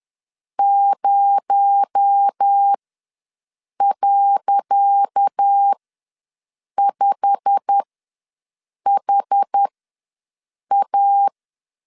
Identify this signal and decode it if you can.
Morse code, heard as a keyed tone, about 10 wpm, spaced as for 6 wpm, 800 Hz. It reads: 0.5HA
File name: LEZIONI / 10 SIGNS / 10D COMA ws 10wpm 4314